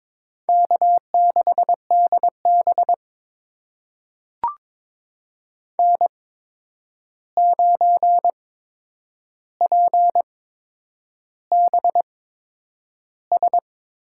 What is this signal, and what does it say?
Morse code, heard as a keyed tone, about 22 wpm, spaced as for 9 wpm, 700 Hz. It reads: K6DB N9PBS
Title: USA Call Signs - 8x Character Spacing - 22wpm